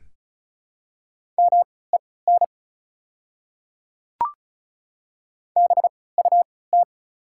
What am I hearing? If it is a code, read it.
Morse code, heard as a keyed tone, about 36 wpm, spaced as for 20 wpm, 690 Hz. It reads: MEN BUT
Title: 3x Character Spacing - Top 200 Words - 35wpm